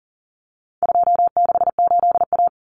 Morse code, heard as a keyed tone, about 40 wpm, 700 Hz. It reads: 268A
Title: Common Cut Numbers - 4-Digit 40wpm